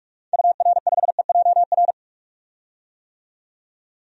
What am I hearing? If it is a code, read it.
Morse code, heard as a keyed tone, about 46 wpm, 700 Hz. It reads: UR5EJR
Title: DXCC-100 Call Signs - 6 Characters - 45wpm